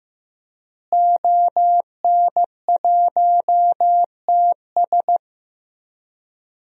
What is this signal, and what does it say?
Morse code, heard as a keyed tone, about 15 wpm, 700 Hz. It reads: ON1TS
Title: DXCC-100 Call Signs - 5 Characters - Repeated 2x - 15wpm